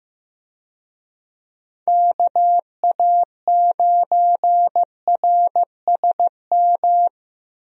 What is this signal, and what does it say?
Morse code, heard as a keyed tone, about 15 wpm, 700 Hz. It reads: KA9RSM
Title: USA Call Signs - 6 Characters - Repeated 3x - 15wpm